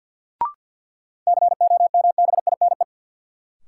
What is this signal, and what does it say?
Morse code, heard as a keyed tone, about 53 wpm, 700 Hz. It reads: COMBINE
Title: Sentences from Top 1600 Words - New Word Drill Alt - 50wpm